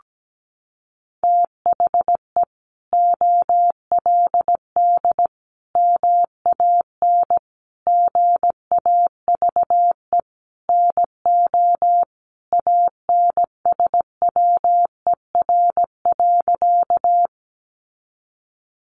Morse code, heard as a keyed tone, about 17 wpm, 700 Hz. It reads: THE OLD MAN GAVE NO ANSWER.
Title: Sentences from Top 500 Words 17wpm